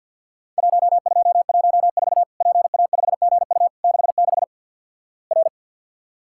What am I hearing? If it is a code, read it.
Morse code, heard as a keyed tone, about 48 wpm, 700 Hz. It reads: 1214 PA5GU 66 R
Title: Worked All Europe DX Contest - QTC Exchange - 50wpm